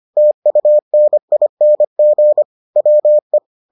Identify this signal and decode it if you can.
Morse code, heard as a keyed tone, about 25 wpm, 600 Hz. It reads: TUNING WE